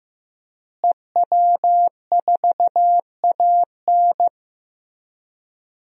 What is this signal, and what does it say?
Morse code, heard as a keyed tone, about 15 wpm, 700 Hz. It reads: EW4AN
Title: DXCC-100 Call Signs - 5 Characters - Repeated 3x - 15wpm